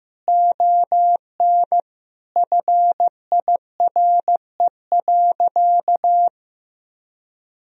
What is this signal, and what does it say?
Morse code, heard as a keyed tone, about 15 wpm, 700 Hz. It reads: ON FIRE.
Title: Sentences from Top 1000 Words - Review Part 1 - 15wpm